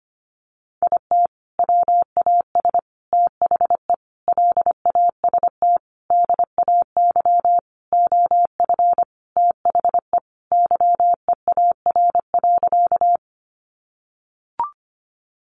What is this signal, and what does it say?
Morse code, heard as a keyed tone, about 25 wpm, 700 Hz. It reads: IT WAS THE LAST DAY OF THE YEAR.
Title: Sentences from Top 300 Words 25wpm